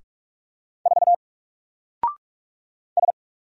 Morse code, heard as a keyed tone, about 44 wpm, 700 Hz. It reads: 4 S
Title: Single Character (Letter:Number:Punctuation:Prosign) Alt 45wpm